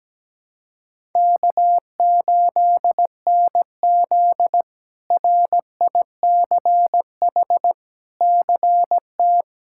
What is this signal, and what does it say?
Morse code, heard as a keyed tone, about 17 wpm, 700 Hz. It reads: K8NZ RICH CT
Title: K1USN SST QSO Practice - 17wpm